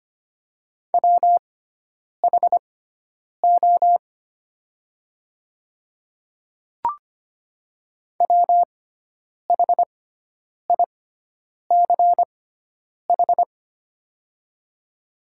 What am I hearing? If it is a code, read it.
Morse code, heard as a keyed tone, about 25 wpm, spaced as for 9 wpm, 700 Hz. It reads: WHO WHICH
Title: 6x Character Spacing - Top 100 Words - 25wpm